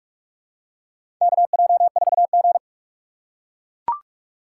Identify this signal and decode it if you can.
Morse code, heard as a keyed tone, about 44 wpm, 700 Hz. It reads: KJ4G
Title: USA Call Signs - 4 Characters - 6x Character Spacing - 45wpm